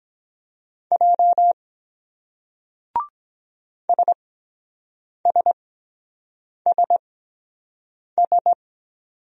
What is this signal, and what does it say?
Morse code, heard as a keyed tone, about 23 wpm, spaced as for 7 wpm, 700 Hz. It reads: J SSSS